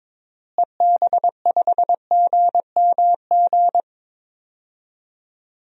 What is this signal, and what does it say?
Morse code, heard as a keyed tone, about 22 wpm, 700 Hz. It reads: EB5GMG
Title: DXCC-100 Call Signs - 6 Characters - Repeated 3x - 22wpm